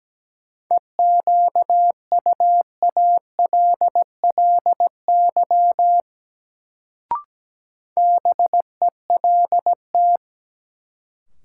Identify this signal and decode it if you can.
Morse code, heard as a keyed tone, about 17 wpm, 700 Hz. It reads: EQUALLY BELT
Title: Sentences from Top 1400 Words - New Word Drill - 17wpm